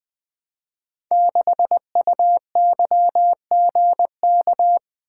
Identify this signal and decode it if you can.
Morse code, heard as a keyed tone, about 20 wpm, 700 Hz. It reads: 6UYGK
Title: Code Groups - 5 Letter:Numbers - 20wpm